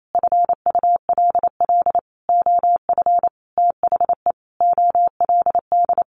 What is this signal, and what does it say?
Morse code, heard as a keyed tone, about 28 wpm, 700 Hz. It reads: FULL OF THE OLD
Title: A Sound Of Thunder 28wpm